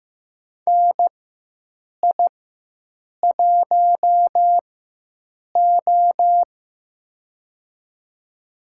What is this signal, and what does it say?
Morse code, heard as a keyed tone, about 15 wpm, spaced as for 7 wpm, 700 Hz. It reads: NI1O
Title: USA Call Signs - 4 Characters - 4x Character Spacing - 15wpm